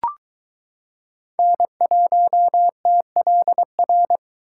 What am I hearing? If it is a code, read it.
Morse code, heard as a keyed tone, about 22 wpm, 700 Hz. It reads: N1TLR